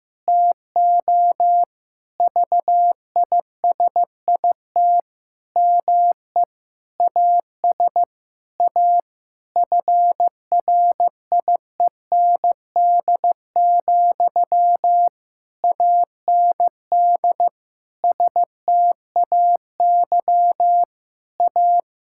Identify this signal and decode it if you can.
Morse code, heard as a keyed tone, about 15 wpm, 700 Hz. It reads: TO VISIT ME AS A FRIEND, AND STAY A
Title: Sentences from Top 1000 Words - Review Part 6 - 15wpm